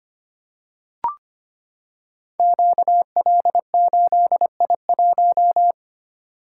Morse code, heard as a keyed tone, about 25 wpm, 700 Hz. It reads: QL8I1
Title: Code Groups - 5 Letter:Numbers - 25wpm